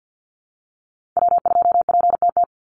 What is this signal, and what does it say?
Morse code, heard as a keyed tone, about 50 wpm, 700 Hz. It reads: W2PTT
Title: Call Signs Repeated 2x - 50wpm